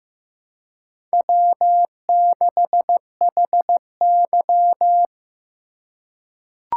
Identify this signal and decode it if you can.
Morse code, heard as a keyed tone, about 15 wpm, 700 Hz. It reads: W6HY
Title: USA Call Signs - 4 Characters - 3x Character Spacing - 15wpm